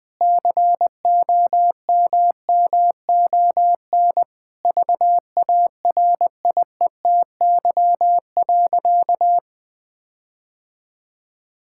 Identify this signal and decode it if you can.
Morse code, heard as a keyed tone, about 20 wpm, 700 Hz. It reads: COMMON VARIETY.
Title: Sentences with 6-Word Phrases - Spoken Beginning - 20wpm